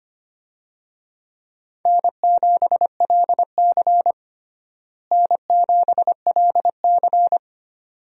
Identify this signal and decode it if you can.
Morse code, heard as a keyed tone, about 25 wpm, 700 Hz. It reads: N7LC N7LC
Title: USA Call Signs - 4 Characters - Repeated 2x - 25wpm